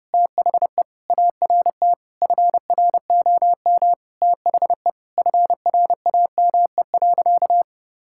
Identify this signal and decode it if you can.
Morse code, heard as a keyed tone, about 30 wpm, 700 Hz. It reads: THE ART FROM THE FRAME.
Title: Sentences from Top 1000 Words - Review Part 6 - 30wpm